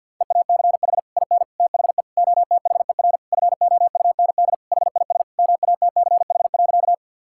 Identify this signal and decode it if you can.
Morse code, heard as a keyed tone, about 50 wpm, 700 Hz. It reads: EACH IN THE OTHER FOUND HIS MATCH.
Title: Sentences from Top 1600 Words 50wpm